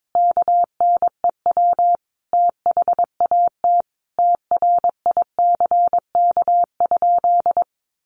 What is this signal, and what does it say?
Morse code, heard as a keyed tone, about 22 wpm, 700 Hz. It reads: KNEW THAT TRICK?
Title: Sentences from Top 1500 Words 22wpm